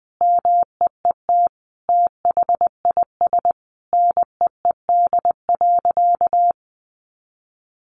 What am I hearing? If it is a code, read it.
Morse code, heard as a keyed tone, about 20 wpm, 700 Hz. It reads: MEET THIS NEED.
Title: Sentences from Top 1100 Words 20wpm